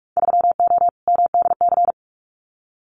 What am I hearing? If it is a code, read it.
Morse code, heard as a keyed tone, about 46 wpm, 700 Hz. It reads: 3O MDC
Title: Winter Field Day Exchanges Encore 45wpm